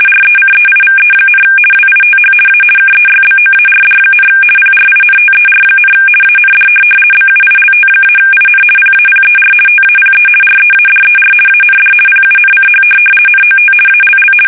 RTTY, about 50 baud. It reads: RECEIVE, WITH A KEYBOARD, PRINTER, AND BUILT-IN PAPER TAPE READER AND PUNCH NOT ALL
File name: fsk850-50B / teletype-corp